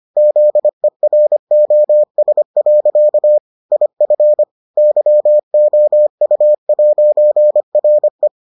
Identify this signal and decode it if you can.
Morse code, heard as a keyed tone, about 25 wpm, 600 Hz. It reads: ZEROS. IF YOU'RE